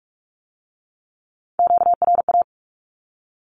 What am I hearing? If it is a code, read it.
Morse code, heard as a keyed tone, about 46 wpm, 700 Hz. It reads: QRA
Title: Q-Codes with Spoken Meaning - 45wpm